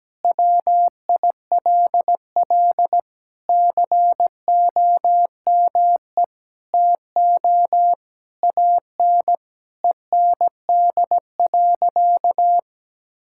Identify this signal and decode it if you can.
Morse code, heard as a keyed tone, about 17 wpm, 700 Hz. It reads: WILL COME TO AN END.
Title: Sentences from Top 1000 Words - Review Part 6 - 17wpm